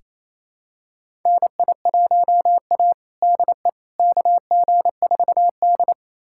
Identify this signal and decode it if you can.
Morse code, heard as a keyed tone, about 28 wpm, 700 Hz. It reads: NI1A DE KG4D
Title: USA Callsign DE Callsign - 4 Characters - 28wpm